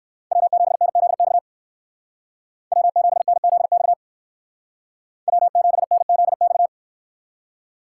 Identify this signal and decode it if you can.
Morse code, heard as a keyed tone, about 68 wpm, 700 Hz. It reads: W7NZX W7NZX W7NZX